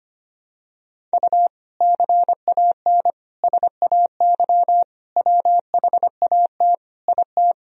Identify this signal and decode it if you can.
Morse code, heard as a keyed tone, about 25 wpm, 700 Hz. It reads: U CAN SAY WHAT IT
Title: Sentences from Top 200 Words and Abbreviations 25wpm